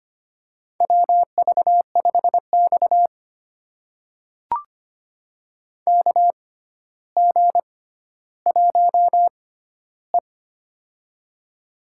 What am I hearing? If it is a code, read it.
Morse code, heard as a keyed tone, about 25 wpm, spaced as for 13 wpm, 700 Hz. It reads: WV5X KG1E
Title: USA Call Signs - 4 Characters - 6x Character Spacing - 25wpm